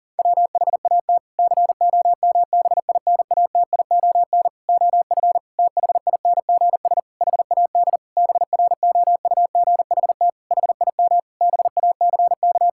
Morse code, heard as a keyed tone, about 40 wpm, 700 Hz. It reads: WHAT COMBINATION OF THINGS HAD BROUGHT HIM BACK
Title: Sentences from Top 1000 Words - Review Part 6 - 40wpm